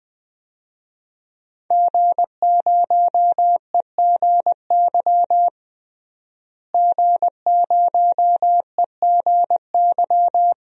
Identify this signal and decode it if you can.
Morse code, heard as a keyed tone, about 21 wpm, 700 Hz. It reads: G0EGY G0EGY
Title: DXCC-100 Call Signs - 5 Characters - Repeated 3x - 20wpm